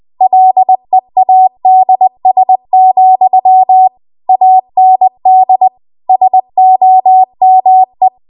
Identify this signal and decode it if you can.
Morse code, heard as a keyed tone, about 20 wpm, 750 Hz. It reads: LEADS, AND SOME